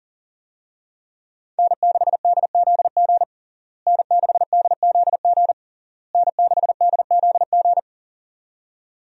Morse code, heard as a keyed tone, about 40 wpm, 700 Hz. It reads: N6DZG N6DZG N6DZG
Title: USA Call Signs - 5 Characters - Repeated 3x - 40wpm